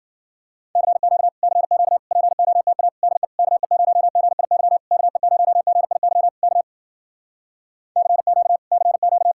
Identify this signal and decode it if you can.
Morse code, heard as a keyed tone, about 52 wpm, 700 Hz. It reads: CQ CQ POTA DE KE0GIY KE0GIY K CQ CQ